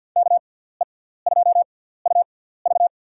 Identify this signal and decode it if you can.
Morse code, heard as a keyed tone, about 51 wpm, spaced as for 17 wpm, 700 Hz. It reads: KE2UV
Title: USA Call Signs - 5 Characters - 6x Character Spacing - 50wpm